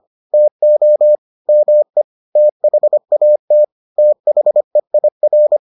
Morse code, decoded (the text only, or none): TO ME THAT THEIR